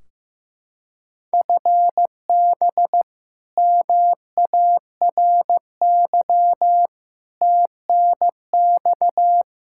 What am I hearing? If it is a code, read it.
Morse code, heard as a keyed tone, about 15 wpm, 700 Hz. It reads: FB MARY TNX
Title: Practice QSO - Exchange 3 - More Info - 15wpm